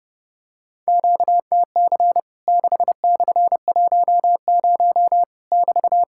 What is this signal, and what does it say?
Morse code, heard as a keyed tone, about 28 wpm, 700 Hz. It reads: QTC 6/10 <BT>
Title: Worked All Europe DX Contest - QTC Exchange - 30wpm